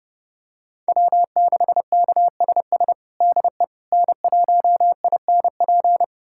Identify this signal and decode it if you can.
Morse code, heard as a keyed tone, about 32 wpm, 700 Hz. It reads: W6KSS DE N1INP
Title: USA Callsign DE Callsign - 5 Characters - 30wpm